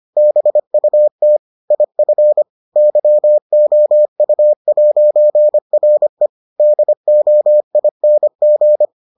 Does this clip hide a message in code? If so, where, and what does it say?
Yes: Morse code from 0.2 to 8.8 s, heard as a keyed tone, about 25 wpm, 600 Hz. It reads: BUT IF YOU'RE DOING